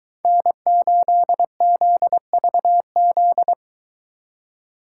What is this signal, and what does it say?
Morse code, heard as a keyed tone, about 23 wpm, 700 Hz. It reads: N8ZVZ